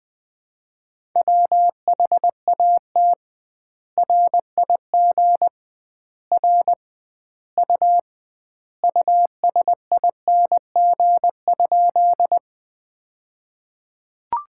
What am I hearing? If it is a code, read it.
Morse code, heard as a keyed tone, about 20 wpm, 700 Hz. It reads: WHAT RIG R U USING?
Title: Common QSO Phrases - 2x Word-Spacing - 20wpm